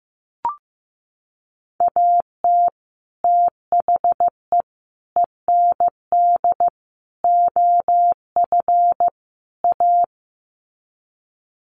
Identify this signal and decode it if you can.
Morse code, heard as a keyed tone, about 15 wpm, 700 Hz. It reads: AT THE END OF A
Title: Sets of 5 Words 15wpm